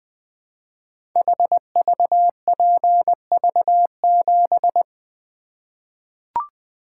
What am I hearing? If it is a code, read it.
Morse code, heard as a keyed tone, about 20 wpm, 700 Hz. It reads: HVPV7
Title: Code Groups - 5 Letter:Numbers - 20wpm